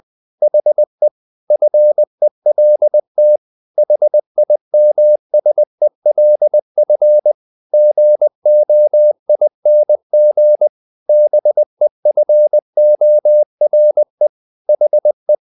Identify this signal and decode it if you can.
Morse code, heard as a keyed tone, about 20 wpm, 600 Hz. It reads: HE FELT HIMSELF GOING BEFORE HE
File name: the old man and the sea part 2 20wpm